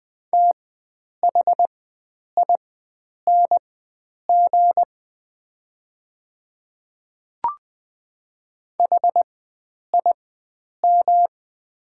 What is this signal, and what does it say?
Morse code, heard as a keyed tone, about 20 wpm, spaced as for 9 wpm, 700 Hz. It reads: THING HIM
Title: 4x Character Spacing - Top 300 Words - 20wpm